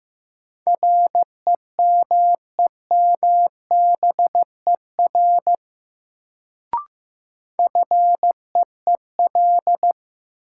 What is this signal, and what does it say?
Morse code, heard as a keyed tone, about 15 wpm, 700 Hz. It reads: REMEMBER FEEL
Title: Sentences from Top 400 Words - New Word Drill - 15wpm